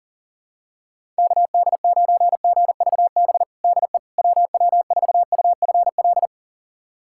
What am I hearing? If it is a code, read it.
Morse code, heard as a keyed tone, about 40 wpm, 700 Hz. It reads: KD9GVB DE WW4UFL
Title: USA Callsign DE Callsign - 6 Characters - 40wpm